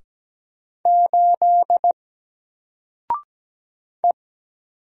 Morse code, heard as a keyed tone, about 17 wpm, 700 Hz. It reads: 8 E
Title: Single Letter-Number Alt 17wpm